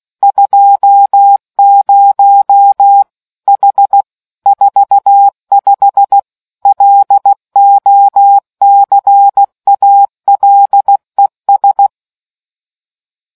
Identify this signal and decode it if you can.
Morse code, heard as a keyed tone, about 16 wpm, 800 Hz. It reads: 20 H 45 LOCALES